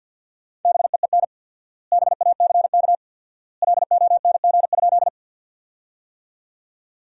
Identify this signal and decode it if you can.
Morse code, heard as a keyed tone, about 53 wpm, 700 Hz. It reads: BEEN BACK LONG?
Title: 4x Word-Spacing - Sentences from Top 200 Words - 50wpm